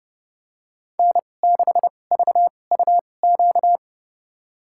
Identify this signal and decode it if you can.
Morse code, heard as a keyed tone, about 30 wpm, spaced as for 22 wpm, 700 Hz. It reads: N6VUQ
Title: USA Call Signs - 2x Character Spacing - 30wpm